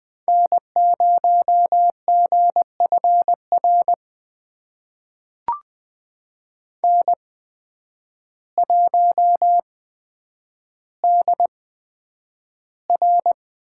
Morse code, heard as a keyed tone, about 20 wpm, spaced as for 9 wpm, 700 Hz. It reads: N0GFR N1DR
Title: USA Call Signs - 5 Characters - 8x Character Spacing - 20wpm